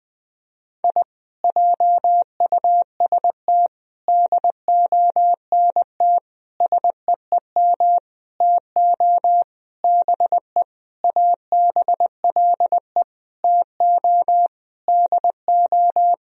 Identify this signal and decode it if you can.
Morse code, heard as a keyed tone, about 21 wpm, 700 Hz. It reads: I JUST DONT SEEM TO BE ABLE TO DO
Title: Sentences from Top 1000 Words - Review Part 2 - 20wpm